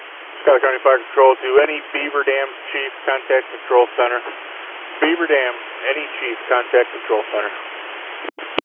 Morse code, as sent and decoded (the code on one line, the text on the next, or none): none